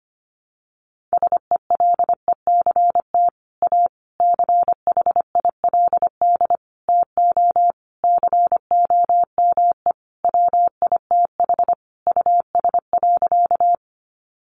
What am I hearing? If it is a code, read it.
Morse code, heard as a keyed tone, about 25 wpm, 700 Hz. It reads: SELECT A CHILD TO COME WITH US.
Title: Sentences from Top 1300 Words 25wpm